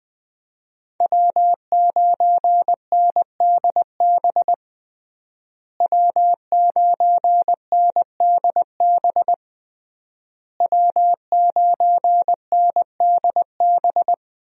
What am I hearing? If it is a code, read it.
Morse code, heard as a keyed tone, about 21 wpm, 700 Hz. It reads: W9NDB W9NDB W9NDB
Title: USA Call Signs - 5 Characters - Repeated 3x - 20wpm